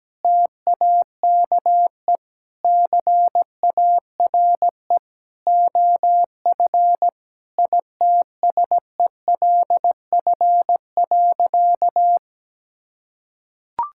Morse code, heard as a keyed tone, about 17 wpm, 700 Hz. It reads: TAKE CARE OF ITSELF.
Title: Sentences from Top 1000 Words - Review Part 5 - 17wpm